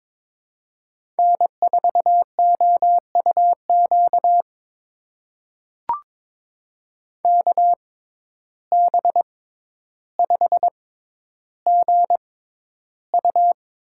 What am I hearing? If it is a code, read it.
Morse code, heard as a keyed tone, about 22 wpm, spaced as for 11 wpm, 700 Hz. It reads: N4OUQ KB5GU
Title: USA Call Signs - 5 Characters - 6x Character Spacing - 22wpm